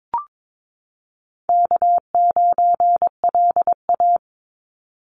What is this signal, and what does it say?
Morse code, heard as a keyed tone, about 22 wpm, 700 Hz. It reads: K9LA
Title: Easy Call Signs Repeated 2x - 22wpm